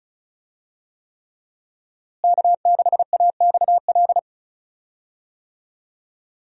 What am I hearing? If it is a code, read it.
Morse code, heard as a keyed tone, about 36 wpm, 700 Hz. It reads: K6AXL
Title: USA Call Signs - 5 Characters - Repeated 3x - 35wpm